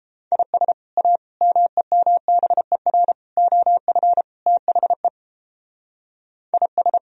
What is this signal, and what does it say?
Morse code, heard as a keyed tone, about 29 wpm, 700 Hz. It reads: IS A MEMBER OF THE IS